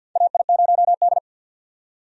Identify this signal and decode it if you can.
Morse code, heard as a keyed tone, about 53 wpm, 700 Hz. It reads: AI0D
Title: USA Call Signs - 4 Characters - 50wpm